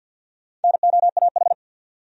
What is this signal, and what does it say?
Morse code, heard as a keyed tone, about 50 wpm, 700 Hz. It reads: NOAH
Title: USA Common Names - Max 4 Letters - 50wpm